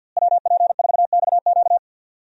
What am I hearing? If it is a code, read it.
Morse code, heard as a keyed tone, about 53 wpm, 700 Hz. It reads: WP4XQ
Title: USA Call Signs - 5 Characters - Repeated 2x - 50wpm